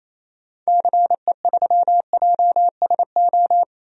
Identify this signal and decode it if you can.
Morse code, heard as a keyed tone, about 28 wpm, 700 Hz. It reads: CE3JSO
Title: DXCC-100 Call Signs - 6 Characters - Repeated 2x - 28wpm